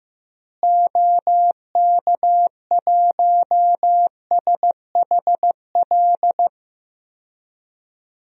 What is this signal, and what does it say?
Morse code, heard as a keyed tone, about 15 wpm, 700 Hz. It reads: OK1SHL
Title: DXCC-100 Call Signs - 6 Characters - Repeated 3x - 15wpm